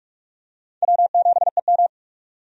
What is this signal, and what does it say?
Morse code, heard as a keyed tone, about 46 wpm, 700 Hz. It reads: W7EM
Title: USA Call Signs - 4 Characters - 45wpm